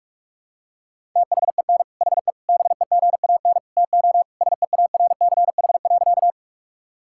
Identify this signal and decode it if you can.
Morse code, heard as a keyed tone, about 41 wpm, 700 Hz. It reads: THEN HE BEGAN TO SEARCH.
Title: Sentences from Top 1000 Words - Review Part 5 - 45wpm